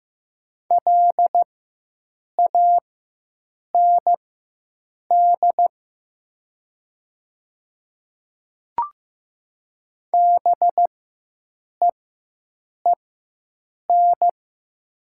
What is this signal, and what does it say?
Morse code, heard as a keyed tone, about 15 wpm, spaced as for 7 wpm, 700 Hz. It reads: LAND BEEN
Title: 4x Character Spacing - Top 200 Words - 15wpm